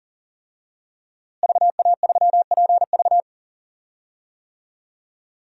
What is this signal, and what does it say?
Morse code, heard as a keyed tone, about 39 wpm, 700 Hz. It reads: VA3PV
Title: DXCC-100 Call Signs - 5 Characters - Repeated 3x - 40wpm